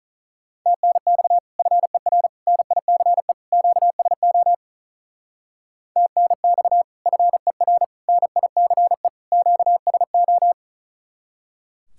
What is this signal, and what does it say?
Morse code, heard as a keyed tone, about 43 wpm, 700 Hz. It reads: TNX FER NICE QSO TNX FER NICE QSO
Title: Speed-Racing Common QSO Phrases Encore - 35wpm